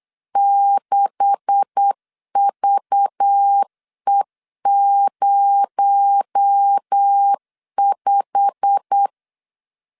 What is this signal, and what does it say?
Morse code, heard as a keyed tone, about 8 wpm, 800 Hz. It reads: 6VE05
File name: CW / DET / 1 9 1500C S B 8wpm 5624